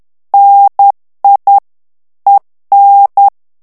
Morse code, heard as a keyed tone, about 10 wpm, 800 Hz. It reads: NI EN
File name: morse mp3 / FRANCAIS TEXTE CLAIR 2 VITESSE 10 MOTS MINUTES